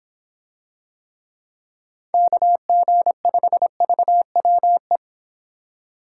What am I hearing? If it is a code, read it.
Morse code, heard as a keyed tone, about 25 wpm, 700 Hz. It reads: KG5VWE